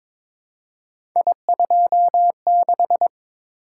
Morse code, heard as a keyed tone, about 22 wpm, 700 Hz. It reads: I26
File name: practice / 22wpm / Three Letter-Number Combinations - 22wpm